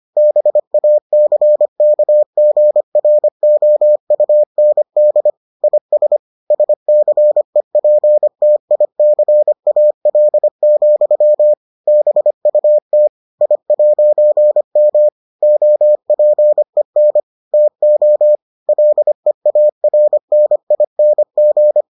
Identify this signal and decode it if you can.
Morse code, heard as a keyed tone, about 25 wpm, 600 Hz. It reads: BACKGROUND IS SCEPTICAL, BUT I'M OPEN TO LEARNING